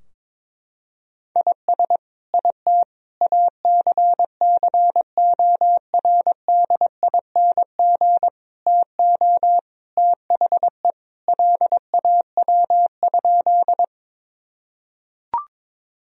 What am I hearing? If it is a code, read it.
Morse code, heard as a keyed tone, about 21 wpm, 700 Hz. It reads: IS IT ACCORDING TO THE LAW?
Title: Sentences from Top 1000 Words - Review Part 4 - 22wpm